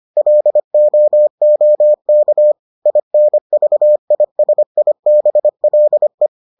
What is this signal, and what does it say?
Morse code, heard as a keyed tone, about 25 wpm, 600 Hz. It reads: LOOK INVISIBLE